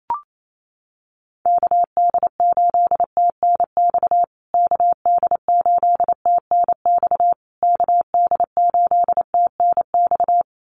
Morse code, heard as a keyed tone, about 28 wpm, 700 Hz. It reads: KD8TNX KD8TNX KD8TNX
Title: USA Call Signs Repeated 3x 28wpm